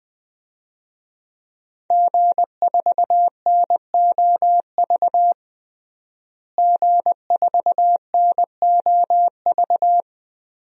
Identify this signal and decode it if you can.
Morse code, heard as a keyed tone, about 20 wpm, 700 Hz. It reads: G4NOV G4NOV
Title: DXCC-100 Call Signs - 5 Characters - Repeated 2x - 20wpm